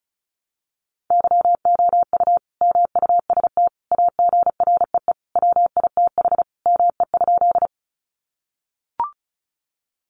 Morse code, heard as a keyed tone, about 34 wpm, 700 Hz. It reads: YOU MUST AGREE WITH ME?
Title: Sentences from Top 1100 Words 35wpm